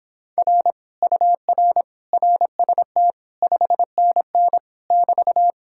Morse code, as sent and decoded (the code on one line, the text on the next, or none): .-. / ..- .-. / .-. ... - / ..... -. -. / -...-
R UR RST 5NN <BT>